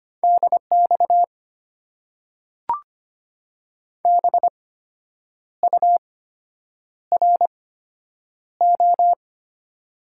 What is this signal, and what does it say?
Morse code, heard as a keyed tone, about 25 wpm, spaced as for 8 wpm, 700 Hz. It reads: DX BURO
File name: Top 90 QSO Elements - 8x Character Spacing - 25wpm